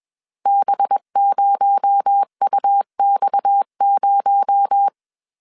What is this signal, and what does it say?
Morse code, heard as a keyed tone, about 21 wpm, 800 Hz. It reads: B0UX0
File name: CW / DET / 1 6 1500C S A 20wpm 2048